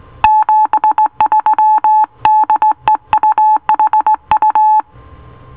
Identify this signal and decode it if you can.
Morse code, heard as a keyed tone, about 18 wpm, 860 Hz. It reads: 73 DE UHU